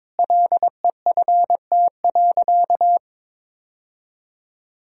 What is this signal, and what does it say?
Morse code, heard as a keyed tone, about 22 wpm, 700 Hz. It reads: LEFT.
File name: practice / 22wpm / Sentences with 3-Word Phrases - 22wpm